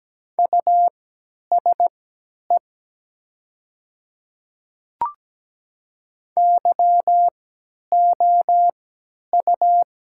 Morse code, heard as a keyed tone, about 17 wpm, spaced as for 10 wpm, 700 Hz. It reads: USE YOU
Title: 3x Character Spacing - Top 100 Words - 17wpm